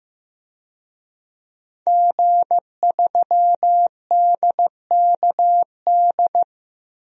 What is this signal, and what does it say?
Morse code, heard as a keyed tone, about 15 wpm, 700 Hz. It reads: G3DKD
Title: DXCC-100 Call Signs - 5 Characters - 15wpm